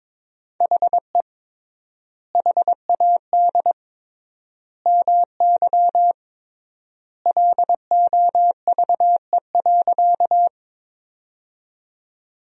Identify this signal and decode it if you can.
Morse code, heard as a keyed tone, about 22 wpm, 700 Hz. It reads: HE HAD MY LOVE.
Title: 3x Word-Spacing - Sentences from Top 400 Words - 22wpm